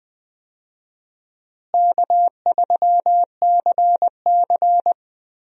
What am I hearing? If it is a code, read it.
Morse code, heard as a keyed tone, about 20 wpm, 700 Hz. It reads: K3CC
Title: USA Call Signs - 4 Characters - Repeated 2x - 20wpm